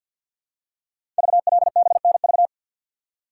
Morse code, heard as a keyed tone, about 52 wpm, 700 Hz. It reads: VLBNV
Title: Code Groups - 5 Letter:Numbers - 50wpm